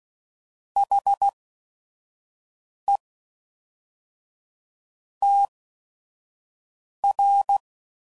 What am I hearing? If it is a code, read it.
Morse code, heard as a keyed tone, about 16 wpm, spaced as for 5 wpm, 800 Hz. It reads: HE TR
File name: CW Code Groups @5WPM / Technical Subjects / AND Gate